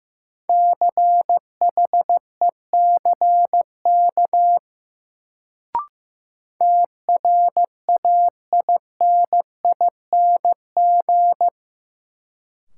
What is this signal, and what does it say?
Morse code, heard as a keyed tone, about 15 wpm, 700 Hz. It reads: CHECK TRAINING
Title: Sentences from Top 1400 Words - New Word Drill Alt - 15wpm